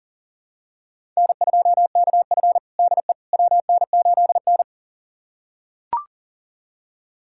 Morse code, heard as a keyed tone, about 40 wpm, 700 Hz. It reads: N2KF DE WN8N
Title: USA Callsign DE Callsign - 4 Characters - 40wpm